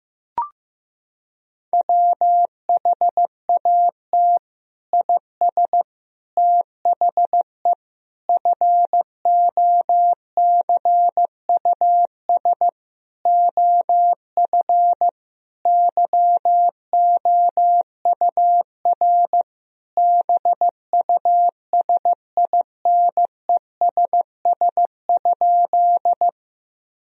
Morse code, heard as a keyed tone, about 15 wpm, 700 Hz. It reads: WHAT IS THE FOCUS OF YOUR BUSINESS?
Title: Sentences from Top 1000 Words - Review Part 4 - 15wpm